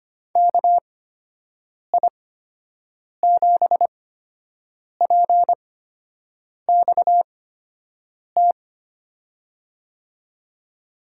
Morse code, heard as a keyed tone, about 25 wpm, spaced as for 7 wpm, 700 Hz. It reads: KI7PXT
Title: USA Call Signs - 6 Characters - 8x Character Spacing - 25wpm